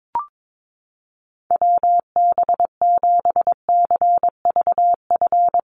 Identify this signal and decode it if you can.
Morse code, heard as a keyed tone, about 22 wpm, 700 Hz. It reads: WB7CVF
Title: USA Call Signs Repeated 2x 22wpm